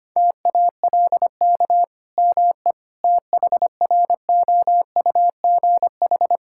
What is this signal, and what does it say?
Morse code, heard as a keyed tone, about 25 wpm, 700 Hz. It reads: TALK ME THROUGH
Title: Sentences from Top 1000 Words - Review Part 5 - 25wpm